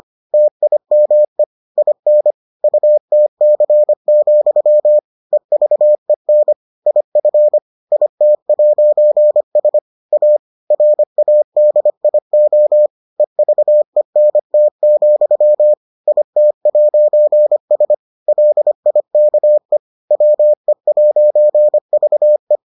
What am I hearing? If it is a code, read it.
Morse code, heard as a keyed tone, about 25 wpm, 600 Hz. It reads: TIME IN UTC, EVEN IF IT'S A RADIO EVENT, IT'S LIKE WE'VE